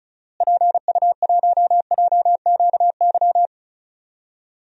Morse code, heard as a keyed tone, about 36 wpm, 700 Hz. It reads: PU1JQY
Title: DXCC-100 Call Signs - 6 Characters - Repeated 3x - 35wpm